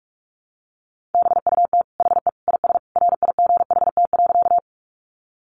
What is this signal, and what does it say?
Morse code, heard as a keyed tone, about 44 wpm, 700 Hz. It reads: BUT HE IS RIGHT.
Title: Sentences from Top 200 Words 45wpm